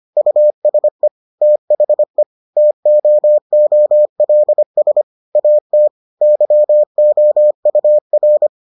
Morse code, heard as a keyed tone, about 25 wpm, 600 Hz. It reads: USE THE TOOLS AT YOUR